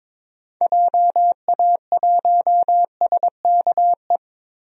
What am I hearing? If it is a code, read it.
Morse code, heard as a keyed tone, about 22 wpm, 700 Hz. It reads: JA1SKE
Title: DXCC-100 Call Signs - 6 Characters - 22wpm